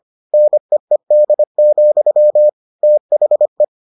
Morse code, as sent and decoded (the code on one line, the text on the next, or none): -. . -.. --..-- / - .... .
NEED, THE